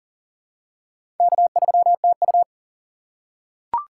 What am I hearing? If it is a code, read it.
Morse code, heard as a keyed tone, about 40 wpm, 700 Hz. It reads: K3TU
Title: USA Call Signs - 4 Characters - 3x Character Spacing - 40wpm